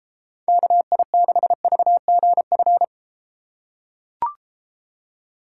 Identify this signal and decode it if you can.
Morse code, heard as a keyed tone, about 32 wpm, 700 Hz. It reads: KI6VGF